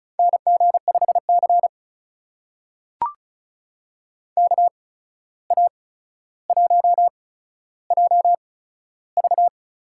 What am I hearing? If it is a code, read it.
Morse code, heard as a keyed tone, about 36 wpm, spaced as for 14 wpm, 700 Hz. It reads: NG5C KA1JV
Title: USA Call Signs - 8x Character Spacing - 35wpm